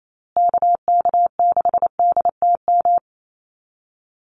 Morse code, heard as a keyed tone, about 28 wpm, 700 Hz. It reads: KK6DTM
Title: USA Call Signs Repeated 3x 28wpm